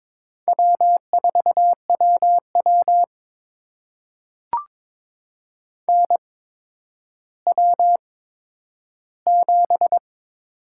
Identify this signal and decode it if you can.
Morse code, heard as a keyed tone, about 22 wpm, spaced as for 11 wpm, 700 Hz. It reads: W4WW NW7
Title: USA Call Signs - 4 Characters - 8x Character Spacing - 22wpm